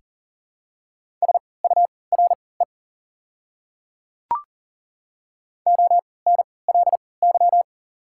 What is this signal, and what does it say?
Morse code, heard as a keyed tone, about 40 wpm, spaced as for 23 wpm, 690 Hz. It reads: SURE ONLY
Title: 3x Character Spacing - Top 300 Words - 40wpm